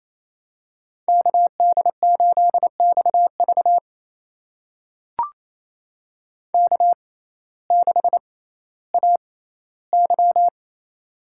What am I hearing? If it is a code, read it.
Morse code, heard as a keyed tone, about 28 wpm, spaced as for 15 wpm, 700 Hz. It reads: KD8XV K6AY
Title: USA Call Signs - 6x Character Spacing - 28wpm